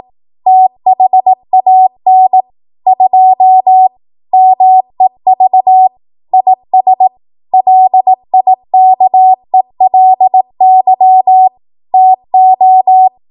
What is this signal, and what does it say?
Morse code, heard as a keyed tone, about 18 wpm, 750 Hz. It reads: THAN 2 MEV IS LIKELY TO
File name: Morse / 18WPM